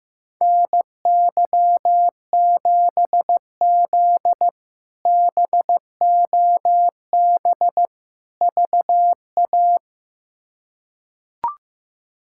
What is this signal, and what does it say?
Morse code, heard as a keyed tone, about 15 wpm, 700 Hz. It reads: NY7Z BOB VA
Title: K1USN SST QSO Practice - 15wpm